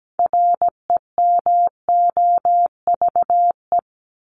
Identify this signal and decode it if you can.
Morse code, heard as a keyed tone, about 17 wpm, 700 Hz. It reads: REMOVE
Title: Top 751-1000 Words 17wpm